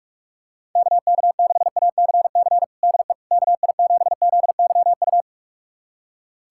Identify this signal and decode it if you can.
Morse code, heard as a keyed tone, about 45 wpm, 700 Hz. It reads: KK6ACC DE KI7ZYU
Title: USA Callsign DE Callsign - 6 Characters - 45wpm